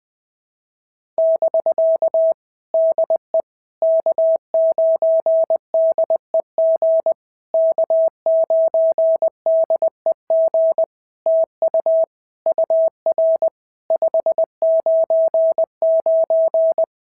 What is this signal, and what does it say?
Morse code, heard as a keyed tone, about 20 wpm, 650 Hz. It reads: <BK> DE K9DEG K9DEG TU UR 599
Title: POTA QSO Practice - 20wpm